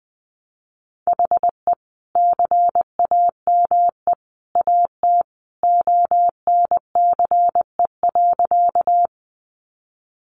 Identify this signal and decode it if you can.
Morse code, heard as a keyed tone, about 20 wpm, 700 Hz. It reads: HE CAME AT ONCE.
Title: Sentences from Top 300 Words 20wpm